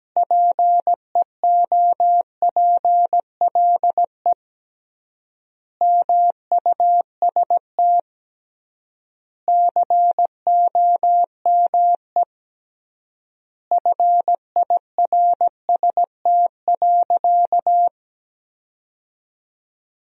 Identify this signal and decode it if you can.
Morse code, heard as a keyed tone, about 17 wpm, 700 Hz. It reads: PEOPLE MUST COME FIRST.
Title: 3x Word-Spacing - Sentences from Top 300 Words - 17wpm